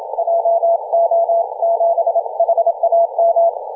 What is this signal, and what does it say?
Morse code, heard as a keyed tone, about 27 wpm, 690 Hz. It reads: JO7HAM
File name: 2016/Sep/25 0323z A35JP 18.085MHz CW